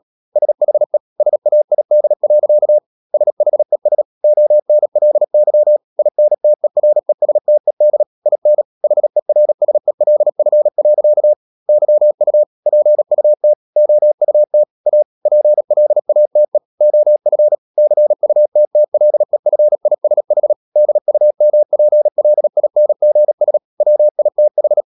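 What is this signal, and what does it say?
Morse code, heard as a keyed tone, about 37 wpm, 600 Hz. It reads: SHE SAID. SHES ONLY INTERESTED IN HERSELF. YU PUT OUT A PLATE OF CUTTLEFISH DUMPLINGS WITH
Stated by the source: Morse code practice at 35–39 wpm sending English text of Ken Follett's novel Never, from CW med Gnister.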